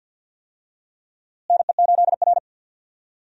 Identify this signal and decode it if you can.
Morse code, heard as a keyed tone, about 52 wpm, 700 Hz. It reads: NE8R
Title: USA Call Signs - 4 Characters - Repeated 3x - 50wpm